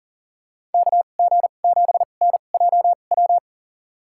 Morse code, heard as a keyed tone, about 40 wpm, spaced as for 29 wpm, 700 Hz. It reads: KG7NJW
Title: USA Call Signs - 2x Character Spacing - 40wpm